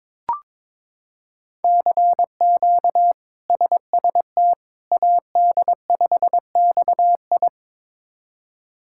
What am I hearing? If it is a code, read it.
Morse code, heard as a keyed tone, about 22 wpm, 700 Hz. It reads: CQ SST AD5XI
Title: K1USN SST QSO Practice - 22wpm